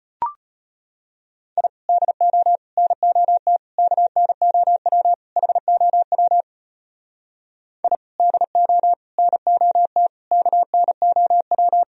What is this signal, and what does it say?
Morse code, heard as a keyed tone, about 32 wpm, 700 Hz. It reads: I DO NOT KNOW HOW I DO NOT KNOW